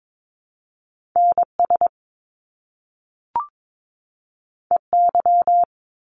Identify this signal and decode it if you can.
Morse code, heard as a keyed tone, about 22 wpm, 700 Hz. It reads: NS EY
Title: Top Two Letter Combinations in Words 22wpm